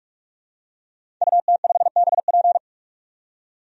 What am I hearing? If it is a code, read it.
Morse code, heard as a keyed tone, about 44 wpm, 700 Hz. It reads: UT5BP
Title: DXCC-100 Call Signs - 5 Characters - 45wpm